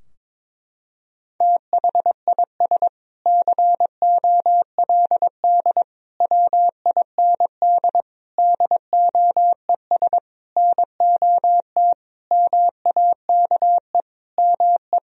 Morse code, heard as a keyed tone, about 22 wpm, 700 Hz. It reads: THIS COLD WIND DOES NOT MAKE ME
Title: Sentences from Top 1000 Words - Review Part 2 - 22wpm